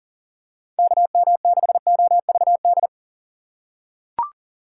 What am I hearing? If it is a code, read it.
Morse code, heard as a keyed tone, about 39 wpm, 700 Hz. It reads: KM6OVD
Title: USA Call Signs - 6 Characters - 6x Character Spacing - 40wpm